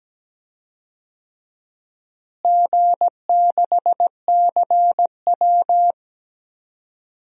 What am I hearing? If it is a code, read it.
Morse code, heard as a keyed tone, about 17 wpm, 700 Hz. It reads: G6CW
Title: DXCC-100 Call Signs - 4 Characters - Repeated 3x - 17wpm